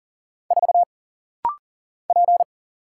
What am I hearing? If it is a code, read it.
Morse code, heard as a keyed tone, about 40 wpm, 700 Hz. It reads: 4 P
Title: Single Letter-Number Alt - Rapid-Fire 40wpm